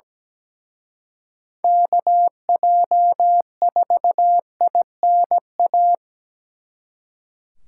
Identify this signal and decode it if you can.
Morse code, heard as a keyed tone, about 20 wpm, 700 Hz. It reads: KJ4INA
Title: Speed Racing - USA 2x3 Call Signs 17wpm